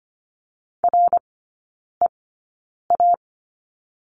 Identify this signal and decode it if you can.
Morse code, heard as a keyed tone, about 25 wpm, spaced as for 9 wpm, 700 Hz. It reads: REA
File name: course / 25:10wpm / 100 Lesson 17e - Letter u - Three Characters 25wpm